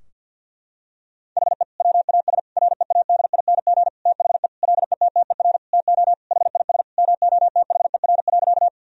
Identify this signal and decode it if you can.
Morse code, heard as a keyed tone, about 50 wpm, 700 Hz. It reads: HE WAS READING THE LETTER TO HIS MOTHER.
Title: Sentences from Top 1000 Words - Review Part 3 - 50wpm